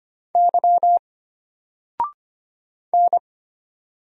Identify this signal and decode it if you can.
Morse code, heard as a keyed tone, about 25 wpm, 700 Hz. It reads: Y N
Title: Single Letter-Number Alt 25wpm